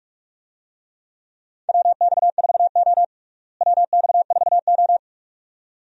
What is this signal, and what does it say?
Morse code, heard as a keyed tone, about 43 wpm, 700 Hz. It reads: WX4O WX4O
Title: USA Call Signs - 4 Characters - Repeated 2x - 45wpm